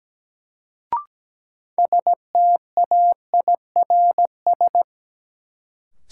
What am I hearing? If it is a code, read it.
Morse code, heard as a keyed tone, about 17 wpm, 700 Hz. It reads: STAIRS
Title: Sets of 5 Words Encore - New Word Drill - 17wpm